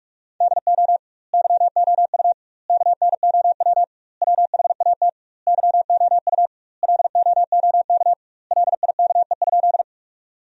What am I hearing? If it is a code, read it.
Morse code, heard as a keyed tone, about 46 wpm, 700 Hz. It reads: DO YOU KNOW WHAT YOU LOOK LIKE?
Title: 2x Word-Spacing - Sentences from Top 100 Words - 45wpm